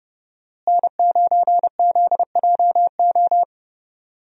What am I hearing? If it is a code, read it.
Morse code, heard as a keyed tone, about 30 wpm, 700 Hz. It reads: N9ZJO
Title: USA Call Signs - 5 Characters - Repeated 3x - 30wpm